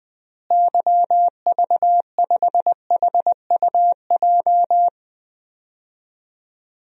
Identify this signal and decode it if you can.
Morse code, heard as a keyed tone, about 20 wpm, 700 Hz. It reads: YV5HUJ
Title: DXCC-100 Call Signs - 6 Characters - Repeated 2x - 20wpm